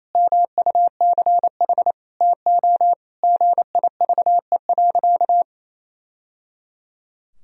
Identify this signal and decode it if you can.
Morse code, heard as a keyed tone, about 28 wpm, 700 Hz. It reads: MUCH TO GIVE.
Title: Sentences from Top 1000 Words - Review Part 4 - 28wpm